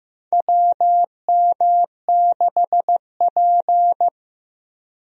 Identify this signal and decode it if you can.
Morse code, heard as a keyed tone, about 15 wpm, 700 Hz. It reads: WM6P
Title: USA Call Signs - 4 Characters - Repeated 3x - 15wpm